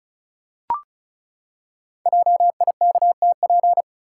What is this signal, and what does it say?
Morse code, heard as a keyed tone, about 36 wpm, 700 Hz. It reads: JIKTP
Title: Code Groups - 5 Letters - 35wpm